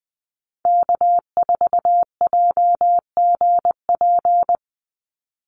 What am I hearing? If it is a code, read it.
Morse code, heard as a keyed tone, about 20 wpm, 700 Hz. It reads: K4JGP
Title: USA Call Signs 20wpm